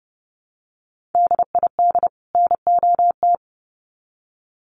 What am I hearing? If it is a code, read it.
Morse code, heard as a keyed tone, about 30 wpm, 700 Hz. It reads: DID NOT
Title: Sets of 2 Words 30wpm